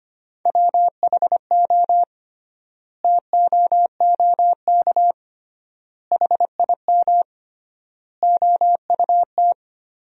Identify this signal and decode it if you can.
Morse code, heard as a keyed tone, about 25 wpm, 700 Hz. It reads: WHO TOOK HIM OUT
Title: 3x Word-Spacing - Sentences from Top 300 Words - 25wpm